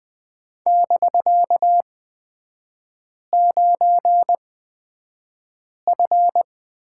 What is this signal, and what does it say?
Morse code, heard as a keyed tone, about 20 wpm, spaced as for 5 wpm, 700 Hz. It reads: <BK>9F
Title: 250 Lesson 40e - BK - Three Characters 20wpm